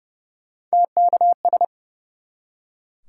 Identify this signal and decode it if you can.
Morse code, heard as a keyed tone, about 40 wpm, spaced as for 35 wpm, 700 Hz. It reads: TKS
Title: Speed-Racing Top 90 QSO Elements 30wpm